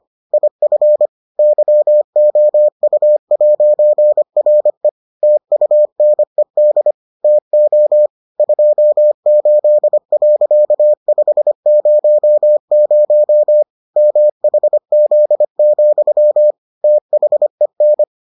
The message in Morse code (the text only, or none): IF YOU'RE TUNED TO 28.500 MHZ, THEN